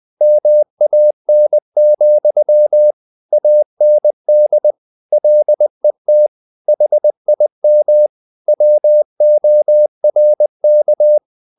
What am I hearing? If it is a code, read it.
Morse code, heard as a keyed tone, about 20 wpm, 600 Hz. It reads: MAN, AND LET HIM WORK
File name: the old man and the sea part 2 20wpm